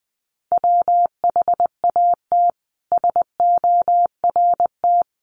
Morse code, heard as a keyed tone, about 20 wpm, 700 Hz. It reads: WHAT SORT
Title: Sentences from Top 1300 Words 20wpm